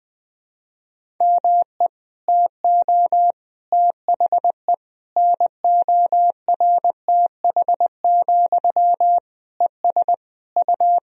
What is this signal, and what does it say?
Morse code, heard as a keyed tone, about 20 wpm, 700 Hz. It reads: ME TO THE NORTH, ES U
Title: Sentences from Top 400 Words and Abbreviations 20wpm